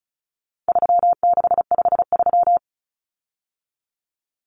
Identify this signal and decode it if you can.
Morse code, heard as a keyed tone, about 34 wpm, 700 Hz. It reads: 3653
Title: Common Cut Numbers - 4-Digit 35wpm